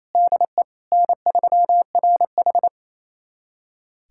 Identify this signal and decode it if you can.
Morse code, heard as a keyed tone, about 28 wpm, 700 Hz. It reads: DE N3RH
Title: USA Callsign DE Callsign - 4 Characters - 28wpm